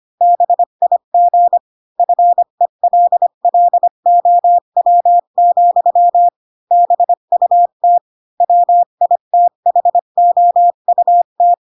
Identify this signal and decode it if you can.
Morse code, heard as a keyed tone, about 25 wpm, 700 Hz. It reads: BIG FELLOW, BUT WITHOUT
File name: Pinocchio / Pinocchio ch3 20 25 ik1hge